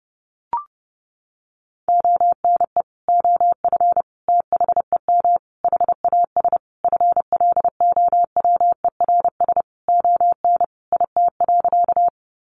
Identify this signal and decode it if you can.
Morse code, heard as a keyed tone, about 30 wpm, 700 Hz. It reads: ONE OF THEM HAS FLOWERS ON IT.
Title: Additional Easy Phrases 30wpm